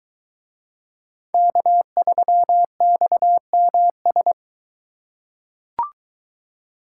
Morse code, heard as a keyed tone, about 23 wpm, 700 Hz. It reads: K3XMS